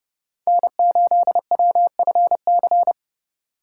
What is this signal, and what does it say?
Morse code, heard as a keyed tone, about 30 wpm, 700 Hz. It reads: N8WFC
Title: USA Call Signs - 5 Characters - 6x Character Spacing - 30wpm